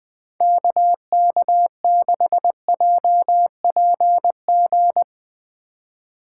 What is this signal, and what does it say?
Morse code, heard as a keyed tone, about 21 wpm, 700 Hz. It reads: KK6JPG